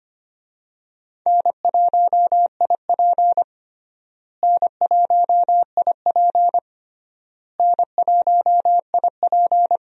Morse code, heard as a keyed tone, about 25 wpm, 700 Hz. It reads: N1IP N1IP N1IP
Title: USA Call Signs - 4 Characters - Repeated 3x - 25wpm